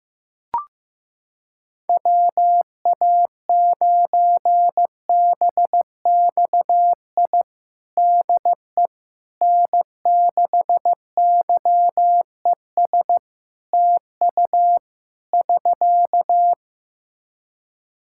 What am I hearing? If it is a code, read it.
Morse code, heard as a keyed tone, about 15 wpm, 700 Hz. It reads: WA9BXI DE N6YES TU <SK>
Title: Practice QSO - Exchange 4 - Ending - 15wpm